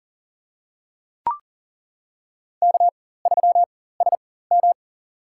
Morse code, heard as a keyed tone, about 40 wpm, spaced as for 19 wpm, 700 Hz. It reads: K3SM
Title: USA Call Signs - 4x Character Spacing - 40wpm